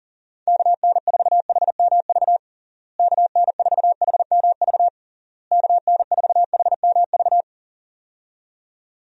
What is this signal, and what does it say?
Morse code, heard as a keyed tone, about 40 wpm, 700 Hz. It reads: KN4HMV KN4HMV KN4HMV
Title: USA Call Signs - 6 Characters - Repeated 3x - 40wpm